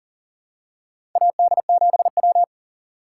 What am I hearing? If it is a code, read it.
Morse code, heard as a keyed tone, about 40 wpm, 700 Hz. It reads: AD7W
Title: USA Call Signs - 4 Characters - 4x Character Spacing - 40wpm